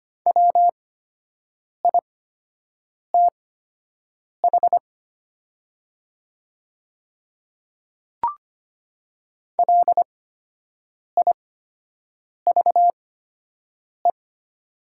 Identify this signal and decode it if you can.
Morse code, heard as a keyed tone, about 25 wpm, spaced as for 7 wpm, 690 Hz. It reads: WITH LIVE
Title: 8x Character Spacing - Top 200 Words - 25wpm